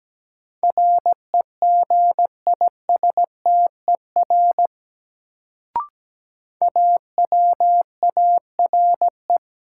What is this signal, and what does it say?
Morse code, heard as a keyed tone, about 17 wpm, 700 Hz. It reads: REGISTER AWARE
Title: Sentences from Top 1200 Words - New Word Drill - 17wpm